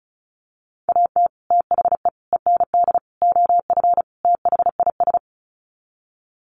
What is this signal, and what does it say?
Morse code, heard as a keyed tone, about 34 wpm, 700 Hz. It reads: AT THE END OF THIS
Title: Sets of 5 Words 35wpm